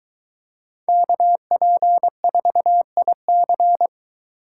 Morse code, heard as a keyed tone, about 23 wpm, 700 Hz. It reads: KP4IC